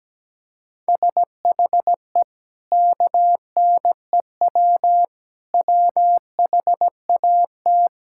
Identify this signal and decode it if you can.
Morse code, heard as a keyed tone, about 17 wpm, 700 Hz. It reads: SHE KNEW WHAT
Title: Sentences from Top 1000 Words - Review Part 5 - 17wpm